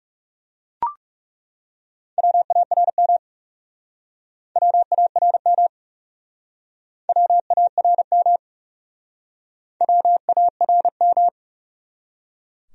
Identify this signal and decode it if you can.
Morse code, heard as a keyed tone, about 37 wpm, 700 Hz. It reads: WARM WARM WARM WARM
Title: Speed-Racing Top 500 Words 30wpm